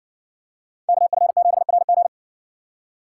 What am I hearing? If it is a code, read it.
Morse code, heard as a keyed tone, about 56 wpm, 700 Hz. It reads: KF7RG